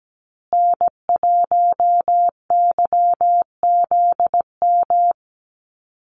Morse code, heard as a keyed tone, about 17 wpm, 700 Hz. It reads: N1YZM
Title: USA Call Signs Repeated 3x 17wpm